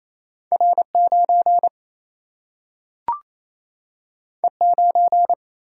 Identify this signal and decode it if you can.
Morse code, heard as a keyed tone, about 28 wpm, 700 Hz. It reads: R 9 E 9
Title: Single-Letter Single-Number 28wpm